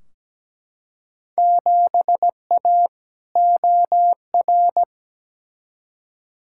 Morse code, heard as a keyed tone, about 17 wpm, 700 Hz. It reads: 7A OR
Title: Field Day Exchanges Encore - 17wpm